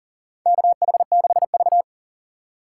Morse code, heard as a keyed tone, about 40 wpm, 700 Hz. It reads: KH6V
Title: USA Call Signs - 4 Characters - 6x Character Spacing - 40wpm